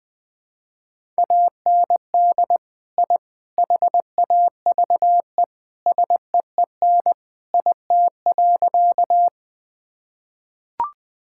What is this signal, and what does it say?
Morse code, heard as a keyed tone, about 20 wpm, 700 Hz. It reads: AND I HAVE SEEN IT.
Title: Sentences with 3-Word Phrases - Spoken Beginning - 20wpm